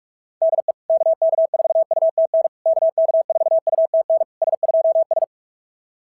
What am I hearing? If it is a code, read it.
Morse code, heard as a keyed tone, about 44 wpm, 690 Hz. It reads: DE KK4UTN KK4UTN S2S
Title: SOTA QSO Practice 45wpm